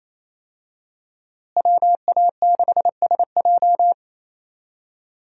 Morse code, heard as a keyed tone, about 28 wpm, 700 Hz. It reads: WA6SJ
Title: USA Call Signs - 5 Characters - Repeated 2x - 28wpm